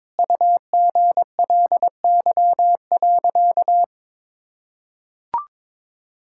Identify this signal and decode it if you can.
Morse code, heard as a keyed tone, about 22 wpm, 700 Hz. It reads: UGLY.
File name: Sentences with 7-Word Phrases - 22wpm